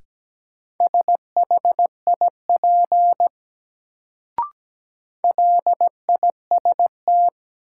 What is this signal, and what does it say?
Morse code, heard as a keyed tone, about 17 wpm, 700 Hz. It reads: SHIP LIST
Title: Sentences from Top 400 Words - New Word Drill - 17wpm